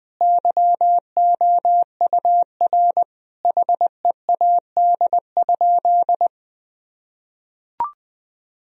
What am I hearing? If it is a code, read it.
Morse code, heard as a keyed tone, about 20 wpm, 700 Hz. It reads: YOUR HEAD?
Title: Sentences from Top 1000 Words - Review Part 1 - 20wpm